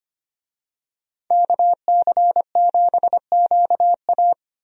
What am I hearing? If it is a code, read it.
Morse code, heard as a keyed tone, about 25 wpm, 700 Hz. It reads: KC7QA
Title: USA Call Signs - 5 Characters - 25wpm